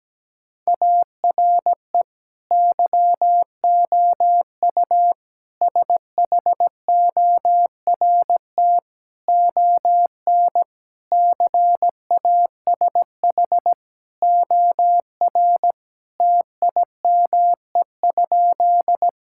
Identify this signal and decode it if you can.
Morse code, heard as a keyed tone, about 17 wpm, 700 Hz. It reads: ARE YOU SHORT ON CASH OR TIME?
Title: Sentences from Top 1000 Words - Review Part 5 - 17wpm